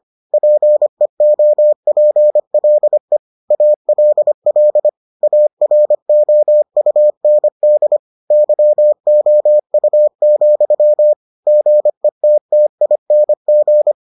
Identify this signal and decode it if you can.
Morse code, heard as a keyed tone, about 25 wpm, 600 Hz. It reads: PEOPLE ALL AROUND YOU, GETTING